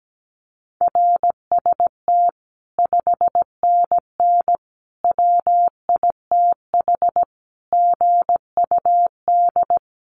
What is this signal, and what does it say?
Morse code, heard as a keyed tone, about 17 wpm, 700 Hz. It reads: RST 5NN WITH GUD
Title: Common QSO Phrases 17wpm